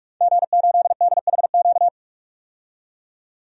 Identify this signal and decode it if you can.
Morse code, heard as a keyed tone, about 46 wpm, 700 Hz. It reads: G8DHQ
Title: DXCC-100 Call Signs - 5 Characters - 45wpm